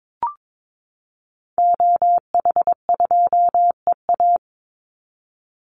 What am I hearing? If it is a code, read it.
Morse code, heard as a keyed tone, about 22 wpm, 700 Hz. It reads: OH2EA
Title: Call Signs Repeated 3x - 22wpm